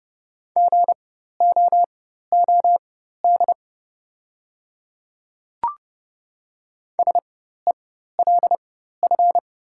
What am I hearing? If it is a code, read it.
Morse code, heard as a keyed tone, about 29 wpm, spaced as for 14 wpm, 700 Hz. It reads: GOOD SELF
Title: 4x Character Spacing - Top 300 Words - 30wpm